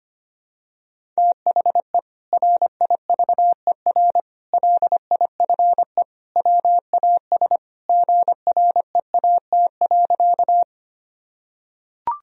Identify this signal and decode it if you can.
Morse code, heard as a keyed tone, about 25 wpm, 700 Hz. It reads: THE RIVER LIFE WAS GREAT.
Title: Sentences from Top 1000 Words - Review Part 1 - 25wpm